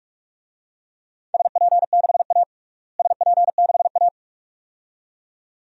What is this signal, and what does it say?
Morse code, heard as a keyed tone, about 46 wpm, 700 Hz. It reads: SP6A SP6A
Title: DXCC-100 Call Signs - 4 Characters - Repeated 2x - 45wpm